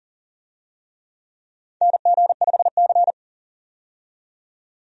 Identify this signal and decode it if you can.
Morse code, heard as a keyed tone, about 37 wpm, 700 Hz. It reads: NG5C
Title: USA Call Signs - 4 Characters - 40wpm